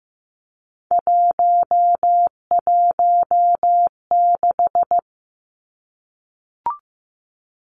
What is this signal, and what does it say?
Morse code, heard as a keyed tone, about 15 wpm, 700 Hz. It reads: 116
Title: Numbers - 3-Digit 15wpm